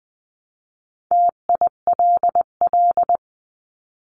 Morse code, heard as a keyed tone, about 20 wpm, 700 Hz. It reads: TILL
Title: 259 Lesson 42d - Words 18-20wpm